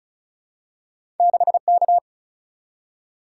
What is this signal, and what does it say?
Morse code, heard as a keyed tone, about 36 wpm, 700 Hz. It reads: 6K
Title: Single-Letter Single-Number 35wpm